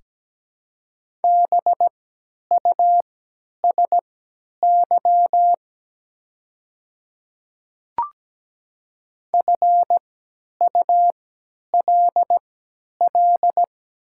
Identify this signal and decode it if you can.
Morse code, heard as a keyed tone, about 17 wpm, spaced as for 10 wpm, 700 Hz. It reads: BUSY FULL
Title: 3x Character Spacing - Top 500 Words - 17wpm